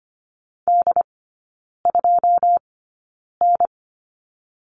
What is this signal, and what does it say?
Morse code, heard as a keyed tone, about 25 wpm, spaced as for 9 wpm, 690 Hz. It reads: D2N
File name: Lesson 14e - Number 2 - Three Characters 25wpm